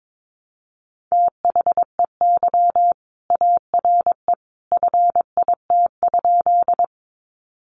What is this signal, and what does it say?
Morse code, heard as a keyed tone, about 22 wpm, 700 Hz. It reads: THEY ARE FIT?
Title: Sentences from Top 800 Words 22wpm